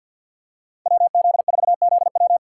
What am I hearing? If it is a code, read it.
Morse code, heard as a keyed tone, about 52 wpm, 700 Hz. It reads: WZ4ZW
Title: USA Call Signs - 5 Characters - 3x Character Spacing - 50wpm